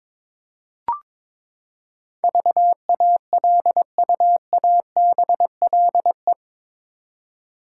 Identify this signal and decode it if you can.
Morse code, heard as a keyed tone, about 22 wpm, 700 Hz. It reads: VALUABLE
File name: practice / 22wpm / Eight Letter Words - 22wpm